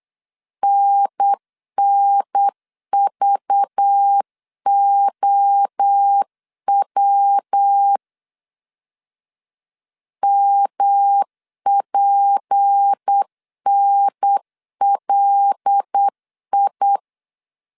Morse code, heard as a keyed tone, about 8 wpm, 800 Hz. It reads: NNVOW MPNLI
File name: 8wpm